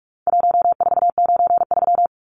Morse code, heard as a keyed tone, about 44 wpm, 700 Hz. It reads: 1493
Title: Numbers - Contest Style - 4-Digit 45wpm